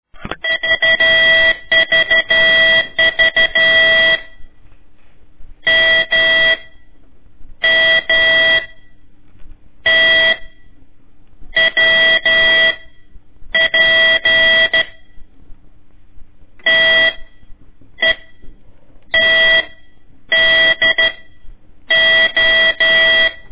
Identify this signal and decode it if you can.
Morse code, heard as a keyed tone, about 13 wpm, 1950 Hz, with noise. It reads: VVV M M T W P T E T D O
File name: ik1weg2 / CORSI CW / lez4-1